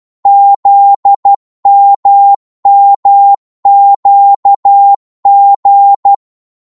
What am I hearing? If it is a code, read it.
Morse code, heard as a keyed tone, about 12 wpm, 800 Hz. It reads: ZMMQG